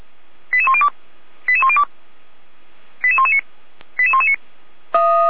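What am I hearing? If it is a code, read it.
Five-tone selective call (ZVEI), heard as quick runs of five tones, tones of about 70 ms. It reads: 88102 88102 88119 88119
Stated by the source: ZVEI five-tone selective call received by SDR.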